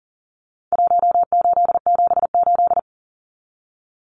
Morse code, heard as a keyed tone, about 40 wpm, 700 Hz. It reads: 1878
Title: Numbers - Contest Style - 4-Digit 40wpm